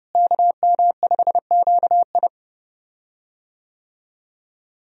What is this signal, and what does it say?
Morse code, heard as a keyed tone, about 30 wpm, 700 Hz. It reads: KM5QI
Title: USA Call Signs - 5 Characters - Repeated 3x - 30wpm